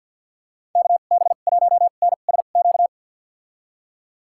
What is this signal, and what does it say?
Morse code, heard as a keyed tone, about 49 wpm, spaced as for 36 wpm, 700 Hz. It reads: KB1NSQ
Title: USA Call Signs - 2x Character Spacing - 50wpm